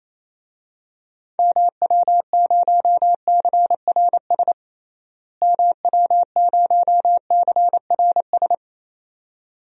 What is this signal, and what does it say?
Morse code, heard as a keyed tone, about 28 wpm, 700 Hz. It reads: MW0CRS MW0CRS
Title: DXCC-100 Call Signs - 6 Characters - Repeated 2x - 28wpm